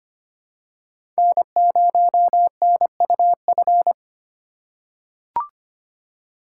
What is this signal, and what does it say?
Morse code, heard as a keyed tone, about 25 wpm, 700 Hz. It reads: N0NUF
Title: USA Call Signs - 4x Character Spacing - 25wpm